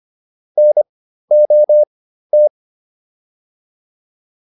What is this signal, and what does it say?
Morse code, heard as a keyed tone, about 25 wpm, spaced as for 13 wpm, 600 Hz. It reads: NOT